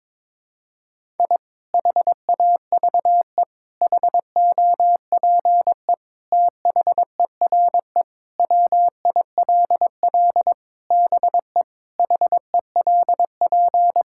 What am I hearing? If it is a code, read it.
Morse code, heard as a keyed tone, about 22 wpm, 700 Hz. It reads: I HAVE HOPE THERE WILL BE HELP
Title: Sentences from Top 1000 Words - Review Part 5 - 22wpm